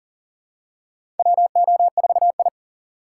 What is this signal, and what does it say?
Morse code, heard as a keyed tone, about 40 wpm, 700 Hz. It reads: WO4I
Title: USA Call Signs - 4 Characters - 4x Character Spacing - 40wpm